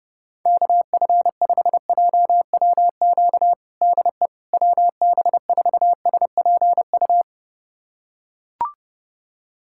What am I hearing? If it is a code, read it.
Morse code, heard as a keyed tone, about 30 wpm, 700 Hz. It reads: KF5JWQ DE WB4SPU
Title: USA Callsign DE Callsign - 6 Characters - 30wpm